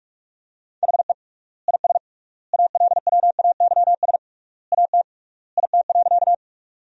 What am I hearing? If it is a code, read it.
Morse code, heard as a keyed tone, about 41 wpm, 700 Hz. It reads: HE IS ALWAYS AT IT.
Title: 3x Word-Spacing - Sentences from Top 300 Words - 45wpm